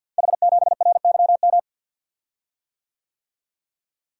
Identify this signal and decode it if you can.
Morse code, heard as a keyed tone, about 49 wpm, 700 Hz. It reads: H7RYM
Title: Code Groups - 5 Letter:Numbers - 50wpm